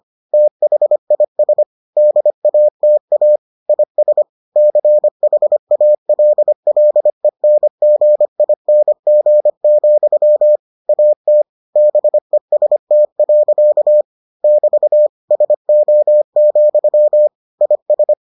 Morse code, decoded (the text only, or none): THIS DATA IS CHALLENGING, AT BEST. <BT> SO, IS